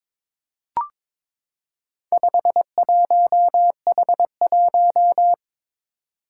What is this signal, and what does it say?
Morse code, heard as a keyed tone, about 22 wpm, 700 Hz. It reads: 51H1
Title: Four Challenging Letter-Number Combinations 22wpm